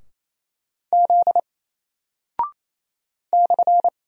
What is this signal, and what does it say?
Morse code, heard as a keyed tone, about 28 wpm, 700 Hz. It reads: Z /
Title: Single Character (Letter:Number:Punctuation:Prosign) Alt 28wpm